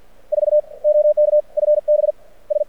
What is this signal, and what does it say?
Morse code, heard as a keyed tone, about 50 wpm, 600 Hz. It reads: 4 OK FB S
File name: archived / CRSnet